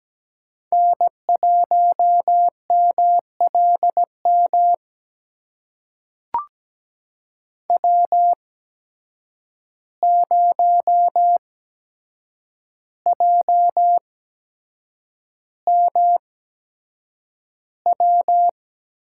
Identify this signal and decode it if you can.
Morse code, heard as a keyed tone, about 17 wpm, spaced as for 7 wpm, 700 Hz. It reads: N1MLM W0JMW
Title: USA Call Signs - 5 Characters - 8x Character Spacing - 17wpm